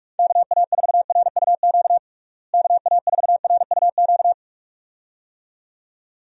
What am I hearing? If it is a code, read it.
Morse code, heard as a keyed tone, about 41 wpm, 700 Hz. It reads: KA4RUQ KA4RUQ
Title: USA Call Signs - 6 Characters - Repeated 2x - 45wpm